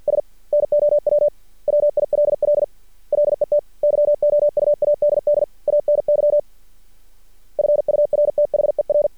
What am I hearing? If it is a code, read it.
Morse code, heard as a keyed tone, about 50 wpm, 590 Hz. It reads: S NOW WILL LET YOU ADD ANY FURTHER
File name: archived / CRSnet